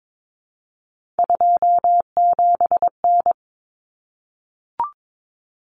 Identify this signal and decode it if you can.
Morse code, heard as a keyed tone, about 22 wpm, 700 Hz. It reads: 27N
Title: Common Cut Numbers - 3-Digit 22wpm